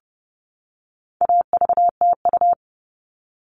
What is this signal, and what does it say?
Morse code, heard as a keyed tone, about 30 wpm, 700 Hz. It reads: AVTU
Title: All Cut Numbers - 4-Digit 30wpm